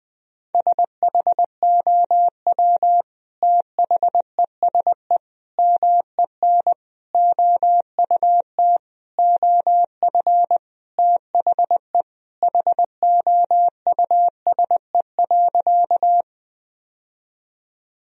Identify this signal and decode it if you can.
Morse code, heard as a keyed tone, about 21 wpm, 700 Hz. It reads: SHOW THESE MEN OUT OF THE HOUSE.
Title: Sentences from Top 1000 Words - Review Part 1 - 20wpm